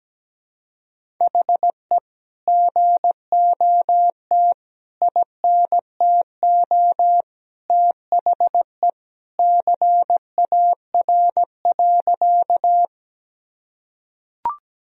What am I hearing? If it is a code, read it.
Morse code, heard as a keyed tone, about 17 wpm, 700 Hz. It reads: HE GOT INTO THE CAR.
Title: Sentences from Top 1000 Words - Review Part 1 - 17wpm